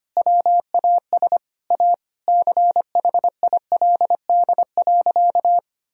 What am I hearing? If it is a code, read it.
Morse code, heard as a keyed tone, about 25 wpm, 700 Hz. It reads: WAS A CHILD.
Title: Sentences from Top 1000 Words - Review Part 3 - 25wpm